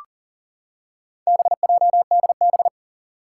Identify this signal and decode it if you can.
Morse code, heard as a keyed tone, about 38 wpm, 700 Hz. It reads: BJDB
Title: Four Challenging Letter-Number Combinations 40wpm